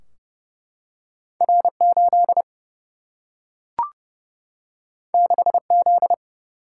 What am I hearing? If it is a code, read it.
Morse code, heard as a keyed tone, about 29 wpm, 700 Hz. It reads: R 8 6 Z
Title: Single-Letter Single-Number 30wpm